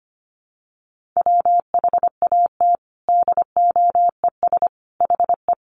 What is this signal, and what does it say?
Morse code, heard as a keyed tone, about 25 wpm, 700 Hz. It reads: WHAT DOES HE
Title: Sentences from Top 1300 Words 25wpm